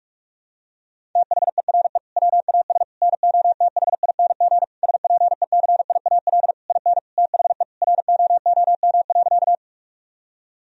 Morse code, heard as a keyed tone, about 46 wpm, 700 Hz. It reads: THERE WAS NOTHING SPECIAL IN THE ROOM.
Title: Sentences from Top 1000 Words - Review Part 3 - 45wpm